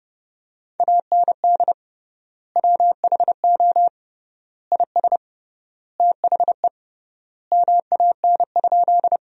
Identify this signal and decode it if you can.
Morse code, heard as a keyed tone, about 32 wpm, 700 Hz. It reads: AND WHO IS THE MAN?
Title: 3x Word-Spacing - Sentences from Top 200 Words - 30wpm